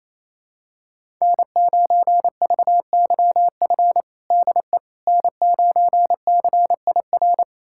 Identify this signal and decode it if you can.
Morse code, heard as a keyed tone, about 27 wpm, 700 Hz. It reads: N9VYF DE N9CIR
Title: USA Callsign DE Callsign - 5 Characters - 28wpm